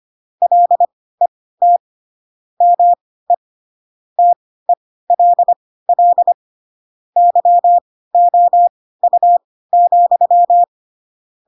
Morse code, heard as a keyed tone, about 25 wpm, spaced as for 16 wpm, 700 Hz. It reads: LET ME TELL YOU,
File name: Pinocchio / Pinocchio ch4 10 25 ik1hge